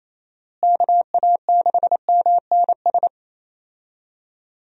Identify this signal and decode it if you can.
Morse code, heard as a keyed tone, about 27 wpm, 700 Hz. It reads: KA6MNS
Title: USA Call Signs - 6 Characters - Repeated 3x - 28wpm